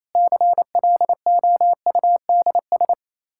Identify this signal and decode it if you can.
Morse code, heard as a keyed tone, about 28 wpm, 700 Hz. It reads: CLOUDS
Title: Practice QSO - Exchange 2 - Info - 28wpm